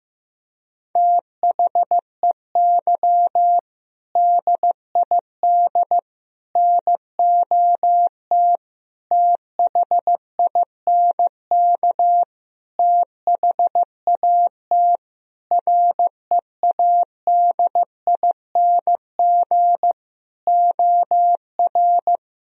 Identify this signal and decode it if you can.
Morse code, heard as a keyed tone, about 15 wpm, 700 Hz. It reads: THEY DID NOT THINK THAT READING OR
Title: Sentences from Top 1000 Words - Review Part 3 - 15wpm